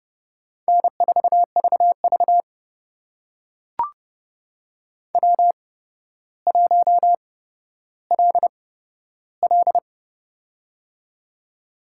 Morse code, heard as a keyed tone, about 30 wpm, spaced as for 13 wpm, 700 Hz. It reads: N4VV W1LL
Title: USA Call Signs - 4 Characters - 8x Character Spacing - 30wpm